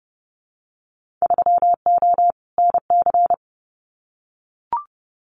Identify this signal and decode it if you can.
Morse code, heard as a keyed tone, about 30 wpm, 700 Hz. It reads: 3O NC
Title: Winter Field Day Exchanges 30wpm